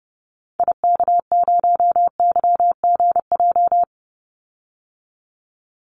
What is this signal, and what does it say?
Morse code, heard as a keyed tone, about 30 wpm, 700 Hz. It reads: IK0YGJ
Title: Call Signs Repeated 2x - 30wpm